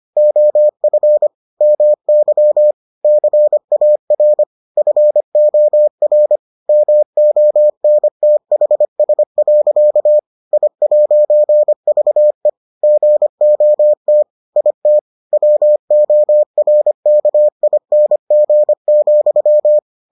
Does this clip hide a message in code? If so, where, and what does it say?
Yes: Morse code from 0.2 to 19.8 s, heard as a keyed tone, about 25 wpm, 600 Hz. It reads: OF MY CAR FOR MONTHS. I'VE GOT IT WORKING,